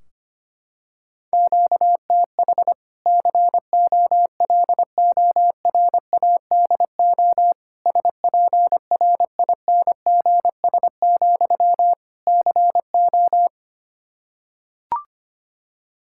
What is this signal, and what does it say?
Morse code, heard as a keyed tone, about 25 wpm, 700 Hz. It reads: QTH COLORADO SPRINGS, CO
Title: Practice QSO - Exchange 1 - Beginning - 25wpm